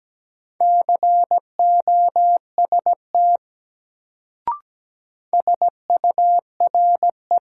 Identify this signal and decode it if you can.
Morse code, heard as a keyed tone, about 17 wpm, 700 Hz. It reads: COST SURE
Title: Sets of 3 Words - New Word Drill - 17wpm